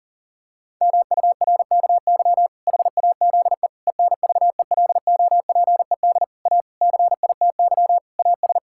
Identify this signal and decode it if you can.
Morse code, heard as a keyed tone, about 40 wpm, 700 Hz. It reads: MURKY HAZE ENVELOPED A CITY AS
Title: Pangram Challenge v2 - 40wpm